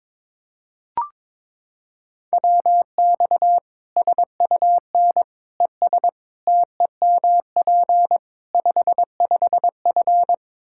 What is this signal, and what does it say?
Morse code, heard as a keyed tone, about 22 wpm, 700 Hz. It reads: WX SUN ES TEMP 55F
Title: K1USN SST QSO Practice 22wpm